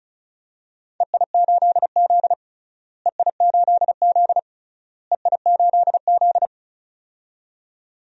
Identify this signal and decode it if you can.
Morse code, heard as a keyed tone, about 34 wpm, 700 Hz. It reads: EI8Z EI8Z EI8Z
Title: DXCC-100 Call Signs - 4 Characters - Repeated 3x - 35wpm